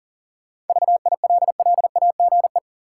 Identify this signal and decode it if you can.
Morse code, heard as a keyed tone, about 40 wpm, 700 Hz. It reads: VILLAGE
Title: Seven Letter Words - 40wpm